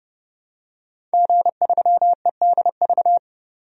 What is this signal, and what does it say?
Morse code, heard as a keyed tone, about 30 wpm, 700 Hz. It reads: G3EDV
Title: DXCC-100 Call Signs - 5 Characters - 30wpm